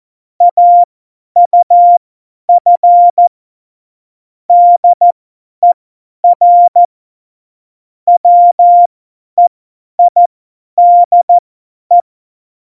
Morse code, heard as a keyed tone, about 14 wpm, spaced as for 10 wpm, 700 Hz. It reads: AUF DER WEIDE
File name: Morsekurs / Witze / Witz 29. Mai 2018, 50 F70